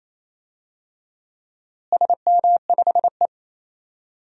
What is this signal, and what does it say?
Morse code, heard as a keyed tone, about 28 wpm, 700 Hz. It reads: SM5E
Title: DXCC-100 Call Signs - 4 Characters - Repeated 2x - 28wpm